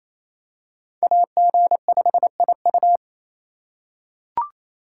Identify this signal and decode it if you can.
Morse code, heard as a keyed tone, about 28 wpm, 700 Hz. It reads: AG5IU
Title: USA Call Signs - 5 Characters - 2x Character Spacing - 28wpm